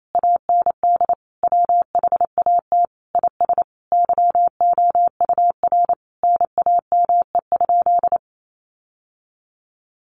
Morse code, heard as a keyed tone, about 28 wpm, 700 Hz. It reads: AND WHAT IS YOUR NAME?
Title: Sentences from Top 1200 Words 28wpm